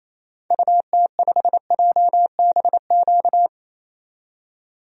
Morse code, heard as a keyed tone, about 28 wpm, 700 Hz. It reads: UT5JBQ
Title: DXCC-100 Call Signs - 6 Characters - 28wpm